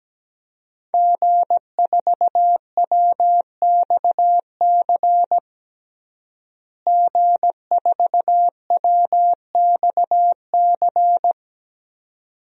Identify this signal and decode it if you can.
Morse code, heard as a keyed tone, about 17 wpm, 700 Hz. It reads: G4WXC G4WXC
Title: DXCC-100 Call Signs - 5 Characters - Repeated 2x - 17wpm